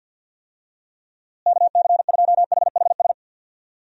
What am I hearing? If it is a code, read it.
Morse code, heard as a keyed tone, about 52 wpm, 700 Hz. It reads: KC2HHS
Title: USA Call Signs - 6 Characters - Repeated 2x - 50wpm